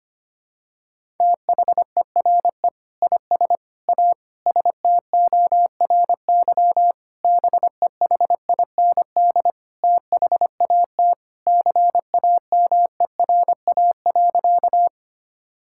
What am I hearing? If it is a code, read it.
Morse code, heard as a keyed tone, about 25 wpm, 700 Hz. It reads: THERE IS A STORY BEHIND THAT CAMERA.
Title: Sentences from Top 1000 Words - Review Part 5 - 25wpm